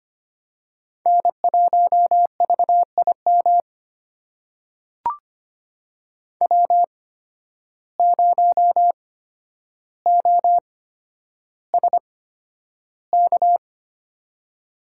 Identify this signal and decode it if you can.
Morse code, heard as a keyed tone, about 25 wpm, spaced as for 11 wpm, 700 Hz. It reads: N1VIM W0OSK
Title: USA Call Signs - 5 Characters - 8x Character Spacing - 25wpm